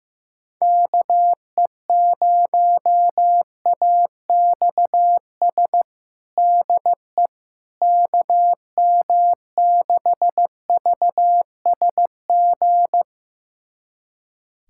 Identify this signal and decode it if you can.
Morse code, heard as a keyed tone, about 15 wpm, 700 Hz. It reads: KE0AXS DE KM6VSG
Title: USA Callsign DE Callsign 15wpm